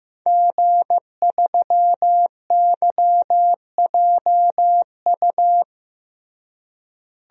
Morse code, heard as a keyed tone, about 15 wpm, 700 Hz. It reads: G3YJU
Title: DXCC-100 Call Signs - 5 Characters - Repeated 3x - 15wpm